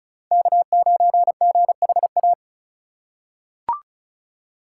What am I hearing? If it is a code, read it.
Morse code, heard as a keyed tone, about 36 wpm, 700 Hz. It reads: K9GHA
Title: USA Call Signs - 5 Characters - 4x Character Spacing - 35wpm